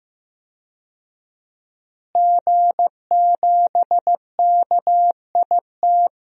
Morse code, heard as a keyed tone, about 15 wpm, 700 Hz. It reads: G7KIT
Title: DXCC-100 Call Signs - 5 Characters - Repeated 2x - 15wpm